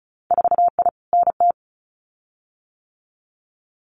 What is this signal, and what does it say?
Morse code, heard as a keyed tone, about 36 wpm, 700 Hz. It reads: 4I NT
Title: Winter Field Day Exchanges Encore 35wpm